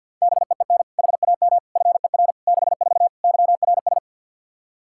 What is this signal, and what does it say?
Morse code, heard as a keyed tone, about 50 wpm, 700 Hz. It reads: BEEN HAM FER 64 YRS
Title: K1USN SST QSO Practice 50wpm